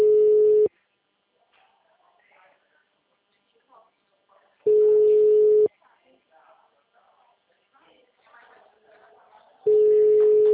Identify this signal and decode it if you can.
Ringback tone (Europe, 425 Hz), with speech.